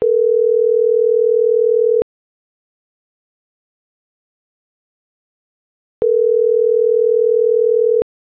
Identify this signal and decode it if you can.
Ringback tone (North America), 440 + 480 Hz.